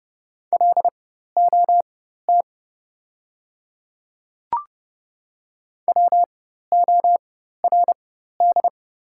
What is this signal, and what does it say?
Morse code, heard as a keyed tone, about 30 wpm, spaced as for 14 wpm, 700 Hz. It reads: LOT WORD
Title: 4x Character Spacing - Top 500 Words - 30wpm